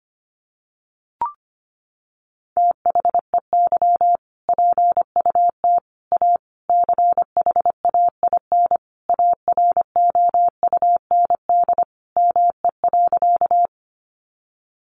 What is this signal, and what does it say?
Morse code, heard as a keyed tone, about 25 wpm, 700 Hz. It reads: THEY PUT A CHAIN AROUND ME.
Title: Sentences from Top 1500 Words 25wpm